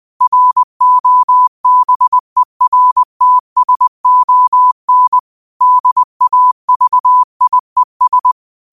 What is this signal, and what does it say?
Morse code, heard as a keyed tone, about 20 wpm, 1000 Hz. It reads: ROBERTSON DAVIES